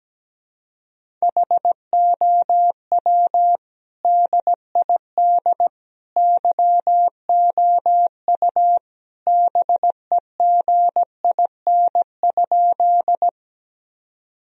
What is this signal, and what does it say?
Morse code, heard as a keyed tone, about 17 wpm, 700 Hz. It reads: HOW DID YOU BEGIN?
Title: Sentences from Top 1000 Words - Review Part 1 - 17wpm